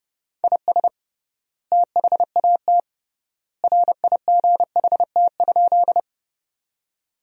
Morse code, heard as a keyed tone, about 32 wpm, 700 Hz. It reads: IS THAT RIGHT?
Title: 3x Word-Spacing - Sentences from Top 200 Words - 30wpm